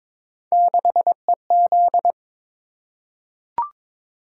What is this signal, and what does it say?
Morse code, heard as a keyed tone, about 22 wpm, 700 Hz. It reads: 6EZ
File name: practice / 22wpm / Three Letter-Number Combinations - 22wpm